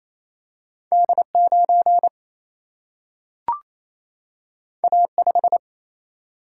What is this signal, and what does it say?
Morse code, heard as a keyed tone, about 28 wpm, 700 Hz. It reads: D9 A5
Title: Single-Letter Single-Number 28wpm